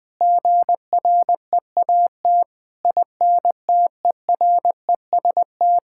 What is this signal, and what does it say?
Morse code, heard as a keyed tone, about 20 wpm, 700 Hz. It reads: GREAT INTEREST
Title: Sentences from Top 1000 Words - Review Part 6 - 20wpm